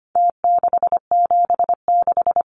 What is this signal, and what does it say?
Morse code, heard as a keyed tone, about 25 wpm, 700 Hz. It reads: T676
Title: Numbers - Contest Style - 4-Digit 25wpm